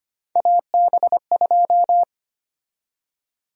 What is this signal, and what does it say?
Morse code, heard as a keyed tone, about 25 wpm, 700 Hz. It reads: AB2
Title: USA Call Sign Prefixes - 25wpm